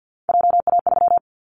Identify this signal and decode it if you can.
Morse code, heard as a keyed tone, about 50 wpm, 700 Hz. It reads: JA3
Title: Callsign Prefixes for Top 25 DXCC - 50wpm